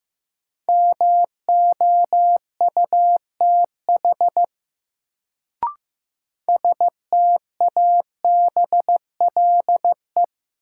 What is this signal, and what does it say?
Morse code, heard as a keyed tone, about 15 wpm, 700 Hz. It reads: MOUTH STABLE
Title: Sentences from Top 1500 Words - New Word Drill Alt - 15wpm